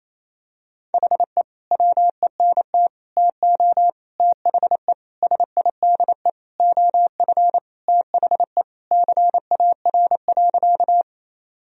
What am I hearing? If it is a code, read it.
Morse code, heard as a keyed tone, about 28 wpm, 700 Hz. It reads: HE WENT TO THE SIDE OF THE CAR.
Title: Sentences from Top 1000 Words - Review Part 1 - 28wpm